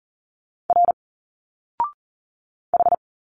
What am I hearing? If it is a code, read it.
Morse code, heard as a keyed tone, about 40 wpm, 700 Hz. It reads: R H
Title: Single Letters 40wpm